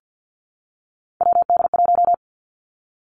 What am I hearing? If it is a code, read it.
Morse code, heard as a keyed tone, about 50 wpm, 700 Hz. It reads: WD1
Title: Three Challenging Letter-Number Combinations 50wpm